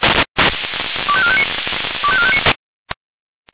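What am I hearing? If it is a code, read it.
Five-tone selective call (ZVEI), heard as quick runs of five tones, tones of about 70 ms. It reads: 25459 25459
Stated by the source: Five-tone ZVEI selective call received by SDR.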